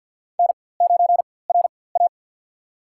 Morse code, heard as a keyed tone, about 53 wpm, spaced as for 24 wpm, 700 Hz. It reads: N9RA
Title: USA Call Signs - 4x Character Spacing - 50wpm